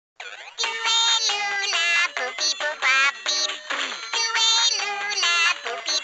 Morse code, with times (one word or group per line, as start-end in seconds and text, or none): none